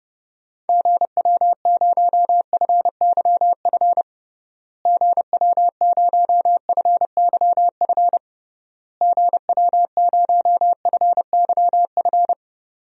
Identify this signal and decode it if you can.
Morse code, heard as a keyed tone, about 29 wpm, 700 Hz. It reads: GW0FYF GW0FYF GW0FYF
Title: DXCC-100 Call Signs - 6 Characters - Repeated 3x - 30wpm